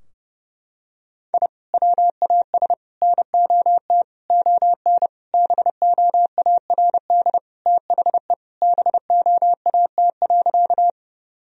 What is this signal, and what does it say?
Morse code, heard as a keyed tone, about 30 wpm, 700 Hz. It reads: I WAS NOT ON BOARD THE BOAT.
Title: Sentences from Top 1000 Words - Review Part 4 - 30wpm